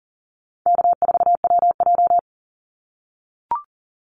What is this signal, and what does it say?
Morse code, heard as a keyed tone, about 37 wpm, 700 Hz. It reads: K4WJ
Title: Call Signs - 40wpm